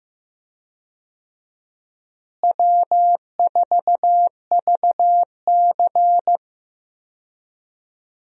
Morse code, heard as a keyed tone, about 15 wpm, 700 Hz. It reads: W4VC
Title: USA Call Signs - 4 Characters - Repeated 3x - 15wpm